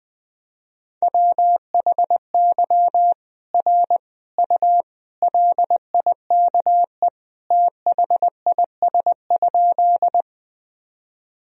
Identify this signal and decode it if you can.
Morse code, heard as a keyed tone, about 20 wpm, 700 Hz. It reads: WHY R U LIKE THIS?
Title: Sentences from Top 200 Words and Abbreviations 20wpm